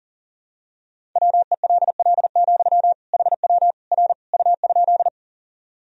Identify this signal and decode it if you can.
Morse code, heard as a keyed tone, about 40 wpm, 700 Hz. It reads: WELL, HW R U?
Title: Sentences from Top 200 Words and Abbreviations 40wpm